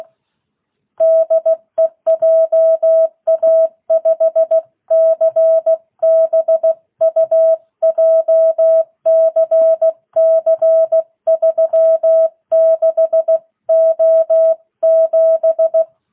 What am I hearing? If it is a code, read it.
Morse code, heard as a keyed tone, about 16 wpm, 650 Hz. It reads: DE JA5CBU JCC 36O7